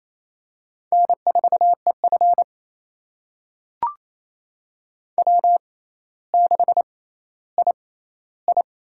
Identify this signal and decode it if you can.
Morse code, heard as a keyed tone, about 28 wpm, spaced as for 14 wpm, 700 Hz. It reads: N4EF W6II
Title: USA Call Signs - 4 Characters - 6x Character Spacing - 28wpm